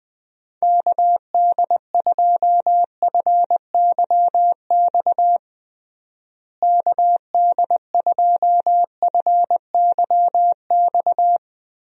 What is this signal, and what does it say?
Morse code, heard as a keyed tone, about 20 wpm, 700 Hz. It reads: KD2FYX KD2FYX
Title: USA Call Signs - 6 Characters - Repeated 2x - 20wpm